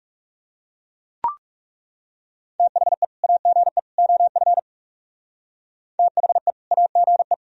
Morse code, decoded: THE AGE OF THE AGE

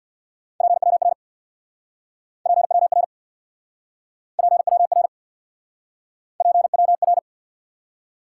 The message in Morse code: PWR PWR PWR PWR

